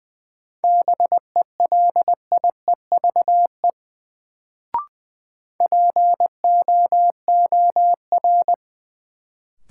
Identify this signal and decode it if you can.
Morse code, heard as a keyed tone, about 20 wpm, 700 Hz. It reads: BELIEVE POOR
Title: Sentences from Top 1000 Words - New Word Drill - 20wpm